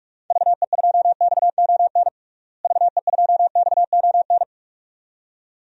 Morse code, heard as a keyed tone, about 46 wpm, 690 Hz. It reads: VE2XON VE2XON
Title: DXCC-100 Call Signs - 6 Characters - Repeated 2x - 45wpm